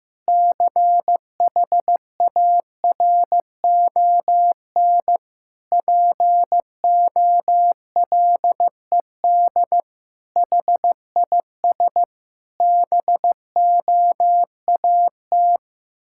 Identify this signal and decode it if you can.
Morse code, heard as a keyed tone, about 16 wpm, 700 Hz. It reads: CHARON POLED HIS BOAT
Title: Pangram Challenge v2 - 15wpm